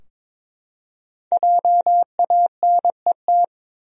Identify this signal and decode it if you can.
Morse code, heard as a keyed tone, about 22 wpm, 700 Hz. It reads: JANET
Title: USA Common Names - Max 5 Letters - 22wpm